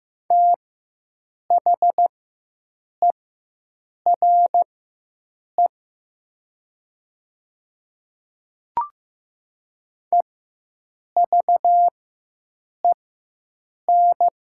Morse code, heard as a keyed tone, about 15 wpm, spaced as for 7 wpm, 700 Hz. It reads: THERE EVEN
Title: 4x Character Spacing - Top 200 Words - 15wpm